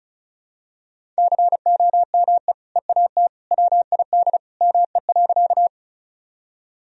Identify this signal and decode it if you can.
Morse code, heard as a keyed tone, about 36 wpm, 700 Hz. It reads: COME EAT WID ME.
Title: Sentences from Top 300 Words and Abbreviations 35wpm